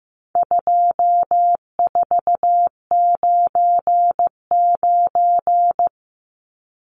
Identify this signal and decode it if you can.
Morse code, heard as a keyed tone, about 15 wpm, 700 Hz. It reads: 2499
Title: Numbers - 4-Digit 15wpm